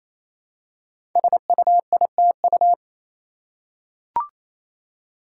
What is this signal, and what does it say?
Morse code, heard as a keyed tone, about 28 wpm, 700 Hz. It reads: SUITU
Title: Code Groups - 5 Letters - 28wpm